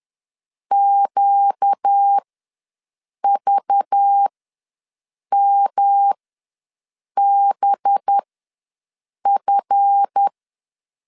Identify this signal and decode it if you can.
Morse code, heard as a keyed tone, about 10 wpm, spaced as for 6 wpm, 800 Hz. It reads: QVMBF
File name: LEZIONI / 10 SIGNS / 10A SLASH ws 10wpm 4121